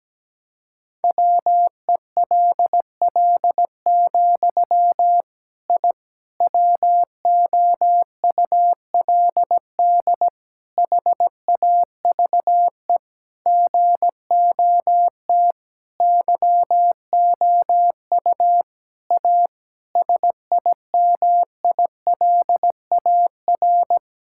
Morse code, heard as a keyed tone, about 17 wpm, 700 Hz. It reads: WELL, I WOULD HAVE GOT YOU A SIMILAR
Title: Sentences from Top 1000 Words - Review Part 5 - 17wpm